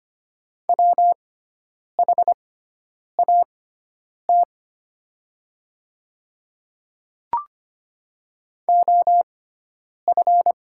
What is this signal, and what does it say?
Morse code, heard as a keyed tone, about 25 wpm, spaced as for 9 wpm, 700 Hz. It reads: WHAT OF
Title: 6x Character Spacing - Top 300 Words - 25wpm